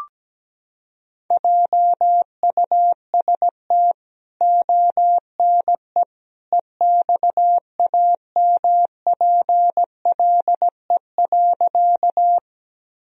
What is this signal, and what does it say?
Morse code, heard as a keyed tone, about 17 wpm, 700 Hz. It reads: JUST ONE EXAMPLE.
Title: Sentences from Top 1000 Words - Review Part 1 - 17wpm